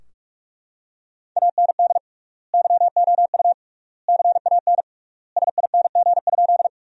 Morse code, heard as a keyed tone, about 42 wpm, 700 Hz. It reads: AND YOU CAN SING?
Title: 3x Word-Spacing - Sentences from Top 400 Words - 45wpm